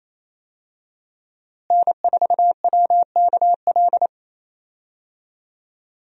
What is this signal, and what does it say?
Morse code, heard as a keyed tone, about 28 wpm, 700 Hz. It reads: N4WKL
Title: USA Call Signs - 5 Characters - Repeated 3x - 28wpm